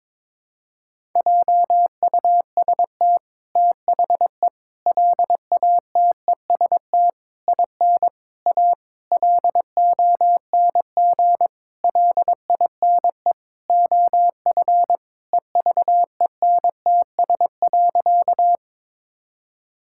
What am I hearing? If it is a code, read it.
Morse code, heard as a keyed tone, about 22 wpm, 700 Hz. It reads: JUST THE LATEST IN A LONG LINE OF EVENTS.
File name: practice / 22wpm / Sentences with 8-Word Phrases - 22wpm